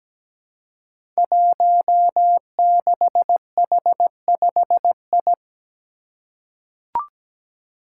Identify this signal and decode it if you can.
Morse code, heard as a keyed tone, about 17 wpm, 700 Hz. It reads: 16H5I
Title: Five Challenging Letter-Number Combinations 17wpm